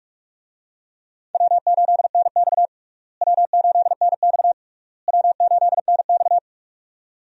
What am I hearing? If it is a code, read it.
Morse code, heard as a keyed tone, about 46 wpm, 700 Hz. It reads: W8NX W8NX W8NX
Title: USA Call Signs - 4 Characters - Repeated 3x - 45wpm